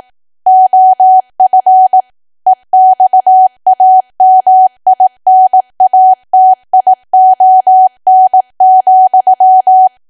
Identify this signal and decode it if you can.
Morse code, heard as a keyed tone, about 18 wpm, 750 Hz. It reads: OF EXAMINATION,
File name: cwa / W1AW-Level2 / 18WPM